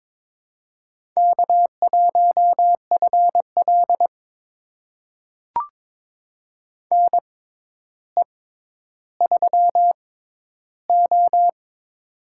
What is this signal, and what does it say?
Morse code, heard as a keyed tone, about 22 wpm, spaced as for 11 wpm, 700 Hz. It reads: K1FL NE3O
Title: USA Call Signs - 4 Characters - 6x Character Spacing - 22wpm